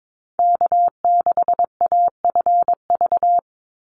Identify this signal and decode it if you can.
Morse code, heard as a keyed tone, about 22 wpm, 700 Hz. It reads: K6AFV
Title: USA Call Signs 22wpm